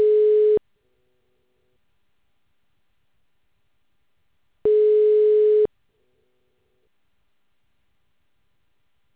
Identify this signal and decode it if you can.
Ringback tone (Europe, 425 Hz).